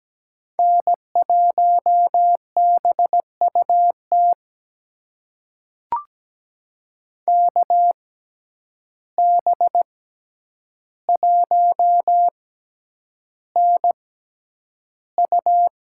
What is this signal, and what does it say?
Morse code, heard as a keyed tone, about 17 wpm, spaced as for 9 wpm, 700 Hz. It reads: N1BUT KB1NU
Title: USA Call Signs - 5 Characters - 6x Character Spacing - 17wpm